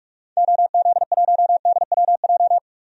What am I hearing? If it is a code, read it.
Morse code, heard as a keyed tone, about 41 wpm, 700 Hz. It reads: OZ1DWJ
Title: DXCC-100 Call Signs - 6 Characters - 45wpm